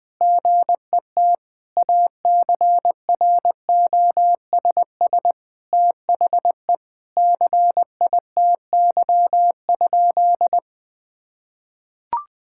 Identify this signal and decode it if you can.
Morse code, heard as a keyed tone, about 20 wpm, 700 Hz. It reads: GET ACROSS THE CITY?
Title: Sentences from Top 1000 Words - Review Part 5 - 20wpm